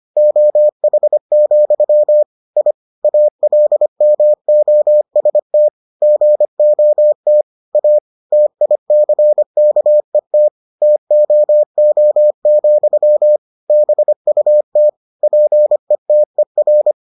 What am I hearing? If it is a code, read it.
Morse code, heard as a keyed tone, about 25 wpm, 600 Hz. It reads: OH, I ALMOST GOT A TICKET TOO, BUT PETER